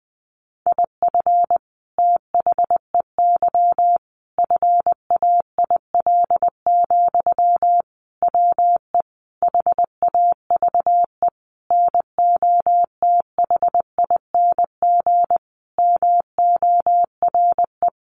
Morse code, heard as a keyed tone, about 20 wpm, 700 Hz. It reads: IF THEY FAIL, WE HAVE NOTHING MORE
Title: Sentences from Top 1100 Words 20wpm